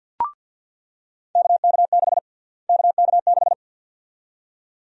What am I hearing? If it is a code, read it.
Morse code, heard as a keyed tone, about 49 wpm, 700 Hz. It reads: KK6 KK6
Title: USA Call Sign Prefixes - Repeated 2x - 50wpm